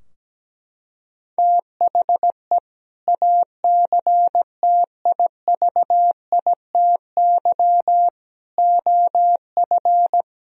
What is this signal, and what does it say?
Morse code, heard as a keyed tone, about 17 wpm, 700 Hz. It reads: THE ACTIVITY OF
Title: Sentences from Top 1000 Words - Review Part 4 - 17wpm